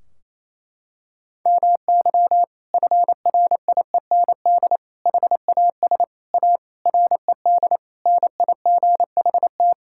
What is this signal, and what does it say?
Morse code, heard as a keyed tone, about 28 wpm, 700 Hz. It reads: MY FRIEND HAS A RED NIGHT
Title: Sentences from Top 1600 Words 28wpm